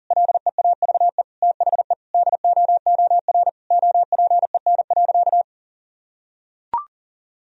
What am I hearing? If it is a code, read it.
Morse code, heard as a keyed tone, about 40 wpm, 700 Hz. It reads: LEAVE THE DOOR OPEN.
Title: Sentences from Top 1000 Words - Review Part 1 - 40wpm